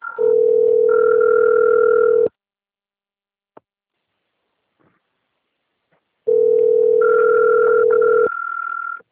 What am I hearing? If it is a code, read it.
Ringback tone (North America), 440 + 480 Hz.
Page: Rio 3